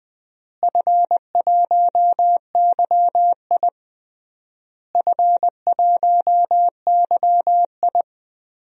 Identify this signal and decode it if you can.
Morse code, heard as a keyed tone, about 20 wpm, 700 Hz. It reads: F1YI F1YI
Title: DXCC-100 Call Signs - 4 Characters - Repeated 3x - 20wpm